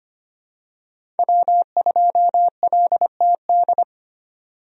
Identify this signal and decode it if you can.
Morse code, heard as a keyed tone, about 25 wpm, 700 Hz. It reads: W2LTD
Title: USA Call Signs - 5 Characters - 25wpm